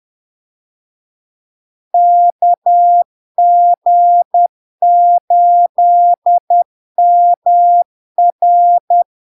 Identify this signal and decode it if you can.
Morse code, heard as a keyed tone, about 10 wpm, 700 Hz. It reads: KG8MR
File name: LARK-FD-2022-Log-in-CW-10wpm-700-Hz